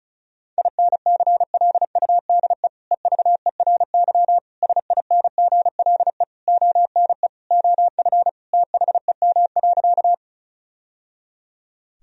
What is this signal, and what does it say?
Morse code, heard as a keyed tone, about 36 wpm, 700 Hz. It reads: INCLUDE EVERY SINGLE ONE OF THEM.
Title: Sentences from Top 1000 Words - Review Part 4 - 35wpm